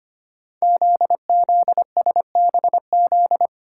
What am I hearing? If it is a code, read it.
Morse code, heard as a keyed tone, about 25 wpm, 700 Hz. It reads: ZZSBZ
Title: Code Groups - 5 Letters - 25wpm